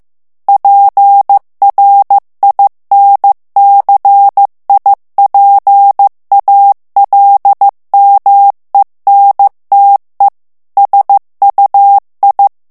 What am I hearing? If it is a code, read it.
Morse code, heard as a keyed tone, about 15 wpm, 800 Hz. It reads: PRINCIPALMENTE SUI
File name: ITALIEN texte clair 1 vitesse 14 mots minute